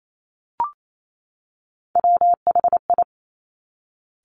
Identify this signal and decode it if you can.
Morse code, heard as a keyed tone, about 28 wpm, 700 Hz. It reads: WHI
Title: Top Three Letter Combinations in Words 28wpm